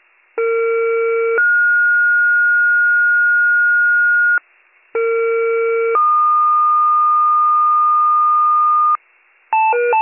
Two-tone page, heard as a one-second tone, then a three-second tone, about 460 then 1430 Hz, about 460 then 1220 Hz, with noise. It is heard from 0.4 to 9.0 s.